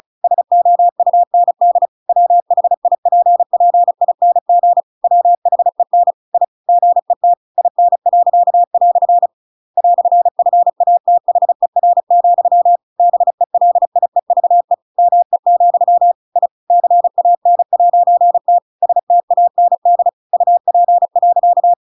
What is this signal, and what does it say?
Morse code, heard as a keyed tone, about 34 wpm, 700 Hz. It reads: SOUND WHIPPING WHEN I GET IN." "FATHER, BELIEVE ME, I CAN'T STAND UP.
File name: Pinocchio / Pinocchio ch7 35 35 ik1hge